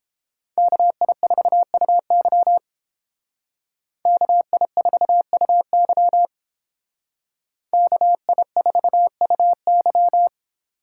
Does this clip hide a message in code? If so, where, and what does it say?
Yes: Morse code from 0.6 to 10.3 s, heard as a keyed tone, about 31 wpm, 700 Hz. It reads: KI4UY KI4UY KI4UY